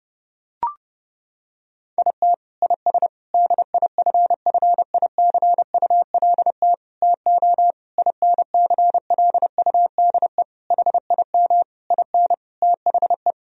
Morse code, heard as a keyed tone, about 30 wpm, 700 Hz. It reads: IT IS DIFFICULT TO INCLUDE HIM IN THE
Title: Sentences from Top 1000 Words - Review Part 3 - 30wpm